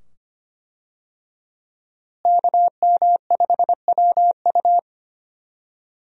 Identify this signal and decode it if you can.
Morse code, heard as a keyed tone, about 25 wpm, 700 Hz. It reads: KM5WU
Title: USA Call Signs - 5 Characters - Repeated 2x - 25wpm